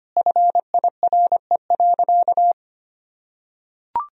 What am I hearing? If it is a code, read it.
Morse code, heard as a keyed tone, about 25 wpm, 700 Hz. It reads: FIRE.
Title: Sentences from Top 1000 Words - Review Part 3 - 25wpm